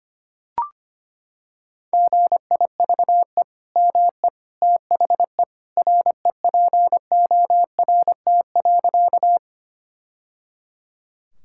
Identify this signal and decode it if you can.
Morse code, heard as a keyed tone, about 25 wpm, 700 Hz. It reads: GIVE ME THE REPORT.
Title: Sentences from Top 1000 Words - Review Part 6 - 25wpm